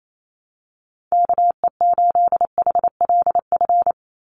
Morse code, heard as a keyed tone, about 28 wpm, 700 Hz. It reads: KE8HLF
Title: USA Call Signs 28wpm